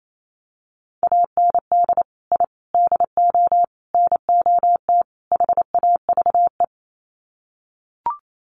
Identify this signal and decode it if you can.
Morse code, heard as a keyed tone, about 28 wpm, 700 Hz. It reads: AND I DO NOT HAVE
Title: Sets of 5 Words Encore 28wpm